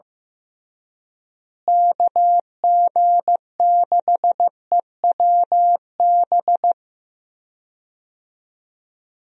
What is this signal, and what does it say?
Morse code, heard as a keyed tone, about 15 wpm, 700 Hz. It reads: KG6EWB
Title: USA Call Signs - 6 Characters - Repeated 3x - 15wpm